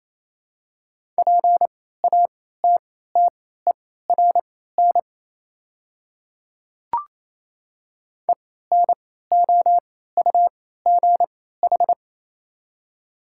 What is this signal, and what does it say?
Morse code, heard as a keyed tone, about 28 wpm, spaced as for 16 wpm, 690 Hz. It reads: PATTERN ENOUGH
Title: 3x Character Spacing - Top 500 Words - 28wpm